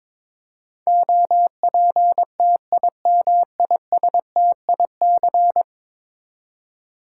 Morse code, heard as a keyed tone, about 22 wpm, 700 Hz. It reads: OPTIMISTIC
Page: morse-code-ninja-practice-22wpm